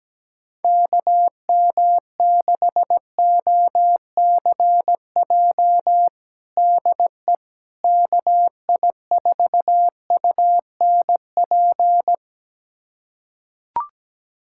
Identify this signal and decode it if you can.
Morse code, heard as a keyed tone, about 17 wpm, 700 Hz. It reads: KM6OCJ DE KI4UNP
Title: USA Callsign DE Callsign - 6 Characters - 17wpm